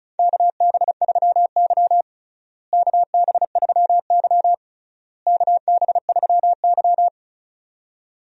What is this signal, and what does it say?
Morse code, heard as a keyed tone, about 36 wpm, 700 Hz. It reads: KB3Y KB3Y KB3Y
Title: USA Call Signs - 4 Characters - Repeated 3x - 35wpm